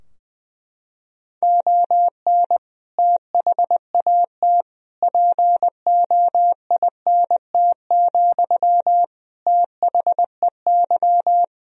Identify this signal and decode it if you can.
Morse code, heard as a keyed tone, about 20 wpm, 700 Hz. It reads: ON THAT POINT, THEY
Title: Sentences from Top 1000 Words - Review Part 4 - 20wpm